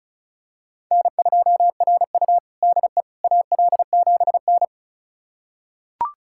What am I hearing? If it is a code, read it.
Morse code, heard as a keyed tone, about 34 wpm, 700 Hz. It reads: N2RU DE AL7N
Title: USA Callsign DE Callsign - 4 Characters - 35wpm